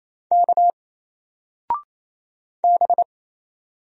Morse code, heard as a keyed tone, about 27 wpm, 700 Hz. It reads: K B
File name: Single-Letter-Number-28wpm